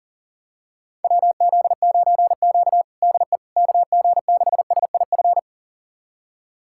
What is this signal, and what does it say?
Morse code, heard as a keyed tone, about 37 wpm, 700 Hz. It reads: WZ9Q DE KG6SIF
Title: USA Callsign DE Callsign 40wpm